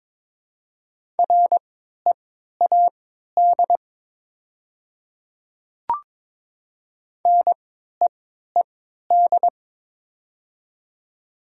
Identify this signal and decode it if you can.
Morse code, heard as a keyed tone, about 21 wpm, spaced as for 12 wpm, 700 Hz. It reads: READ NEED
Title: 3x Character Spacing - Top 300 Words - 22wpm